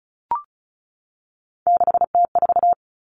Morse code, heard as a keyed tone, about 34 wpm, 700 Hz. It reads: 6T4
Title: Three Challenging Letter-Number Combinations 35wpm